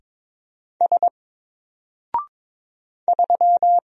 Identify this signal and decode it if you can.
Morse code, heard as a keyed tone, about 22 wpm, 700 Hz. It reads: S 3
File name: Single-Letter-Number-22wpm